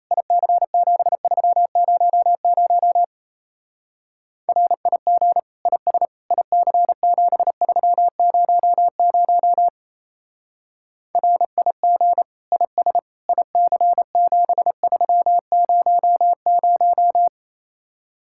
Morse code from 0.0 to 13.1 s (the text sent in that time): IC7300 RIG IS IC7300 RIG IS